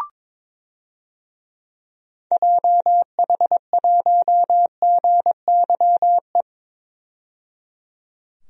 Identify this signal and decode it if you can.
Morse code, heard as a keyed tone, about 22 wpm, 700 Hz. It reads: JH1GYE
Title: DXCC-100 Call Signs - 6 Characters - 22wpm